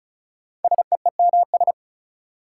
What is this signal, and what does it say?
Morse code, heard as a keyed tone, about 36 wpm, 700 Hz. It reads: SEEMS
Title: Sets of 3 Words - New Word Drill - 35wpm